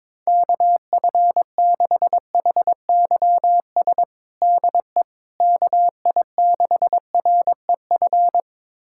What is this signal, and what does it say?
Morse code, heard as a keyed tone, about 22 wpm, 700 Hz. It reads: KF6HYS DE KI6REF
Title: USA Callsign DE Callsign - 6 Characters - 22wpm